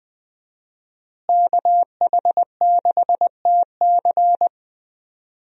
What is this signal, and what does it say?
Morse code, heard as a keyed tone, about 20 wpm, 700 Hz. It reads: KH6TC
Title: DXCC-100 Call Signs - 5 Characters - Repeated 2x - 20wpm